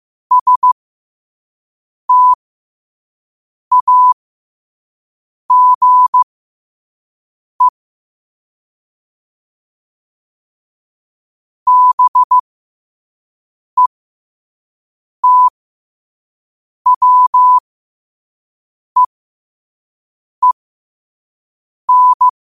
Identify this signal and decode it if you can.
Morse code, heard as a keyed tone, about 15 wpm, spaced as for 5 wpm, 1000 Hz. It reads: STAGE BETWEEN
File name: qotdfiles / 5wpm